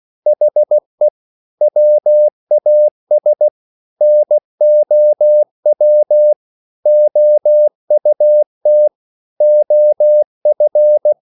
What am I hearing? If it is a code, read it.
Morse code, heard as a keyed tone, about 16 wpm, 600 Hz. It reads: HE WAS NOW OUT OF